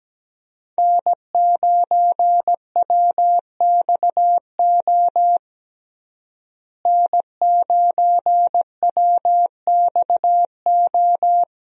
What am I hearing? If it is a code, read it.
Morse code, heard as a keyed tone, about 17 wpm, 700 Hz. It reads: N9WXO N9WXO
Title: USA Call Signs - 5 Characters - Repeated 3x - 17wpm